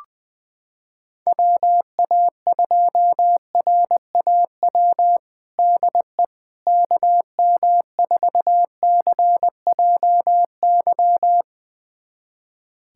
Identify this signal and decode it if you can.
Morse code, heard as a keyed tone, about 20 wpm, 700 Hz. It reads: WA2RAW DE KM4CJY
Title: USA Callsign DE Callsign - 6 Characters - 20wpm